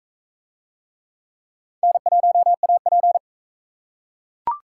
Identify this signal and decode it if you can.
Morse code, heard as a keyed tone, about 42 wpm, 700 Hz. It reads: N1AP